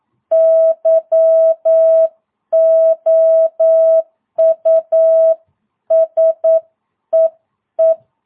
Morse code, heard as a keyed tone, about 9 wpm, 650 Hz. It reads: YOU SEE
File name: jh8cbh / 2022cw